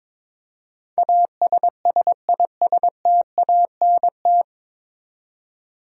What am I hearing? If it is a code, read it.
Morse code, heard as a keyed tone, about 22 wpm, 700 Hz. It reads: ASSISTANT
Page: Morse Code Ninja practice 22wpm